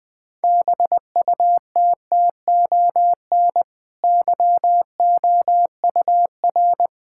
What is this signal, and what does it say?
Morse code, heard as a keyed tone, about 20 wpm, 700 Hz. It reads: BUTTON YOUR
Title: Easy Sentences with Hints - 20wpm